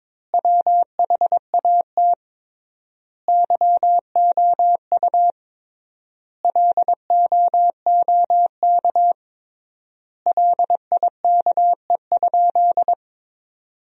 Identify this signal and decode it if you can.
Morse code, heard as a keyed tone, about 22 wpm, 700 Hz. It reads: WHAT YOU LOOK LIKE?
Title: 3x Word-Spacing - Sentences from Top 100 Words - 22wpm